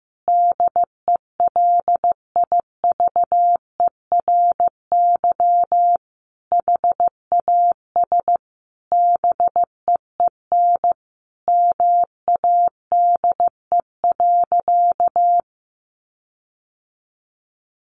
Morse code, heard as a keyed tone, about 15 wpm, 700 Hz. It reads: DELIVERY HAS BEEN MADE.
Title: Sentences from Top 1500 Words 15wpm